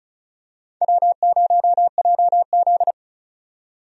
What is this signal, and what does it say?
Morse code, heard as a keyed tone, about 36 wpm, 700 Hz. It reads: W0JZ
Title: USA Call Signs - 4 Characters - 35wpm